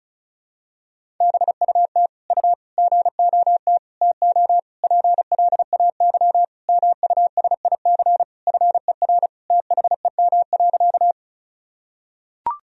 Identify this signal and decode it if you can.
Morse code, heard as a keyed tone, about 36 wpm, 700 Hz. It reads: BUT U GOT TO PLAY MUSIC FER THEM.
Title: Sentences from Top 300 Words and Abbreviations 35wpm